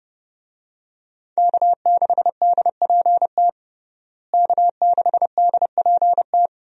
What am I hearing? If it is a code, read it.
Morse code, heard as a keyed tone, about 30 wpm, 700 Hz. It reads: K6DPT K6DPT
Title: USA Call Signs - 5 Characters - Repeated 2x - 30wpm